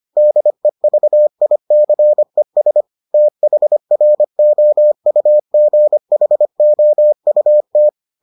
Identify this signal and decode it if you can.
Morse code, heard as a keyed tone, about 25 wpm, 600 Hz. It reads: DEVICES THROUGHOUT